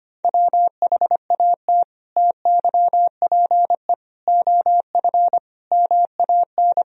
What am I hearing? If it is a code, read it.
Morse code, heard as a keyed tone, about 25 wpm, 700 Hz. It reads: WHAT TYPE OF MAN
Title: Sentences from Top 1000 Words - Review Part 3 - 25wpm